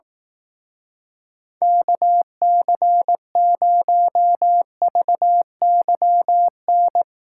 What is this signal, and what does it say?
Morse code, heard as a keyed tone, about 18 wpm, 700 Hz. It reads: KC0VYN
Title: Speed Racing - USA 2x3 Call Signs 15wpm